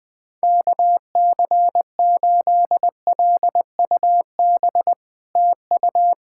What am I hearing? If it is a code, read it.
Morse code, heard as a keyed tone, about 21 wpm, 700 Hz. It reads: KC8LUB TU
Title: POTA QSO Practice - 20wpm